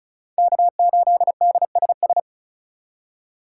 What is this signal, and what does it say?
Morse code, heard as a keyed tone, about 36 wpm, 700 Hz. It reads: K8DSS
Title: K1USN SST QSO Practice - 35wpm